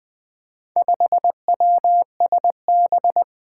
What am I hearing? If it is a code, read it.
Morse code, heard as a keyed tone, about 20 wpm, 700 Hz. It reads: 5WSB
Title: Four Challenging Letter-Number Combinations 20wpm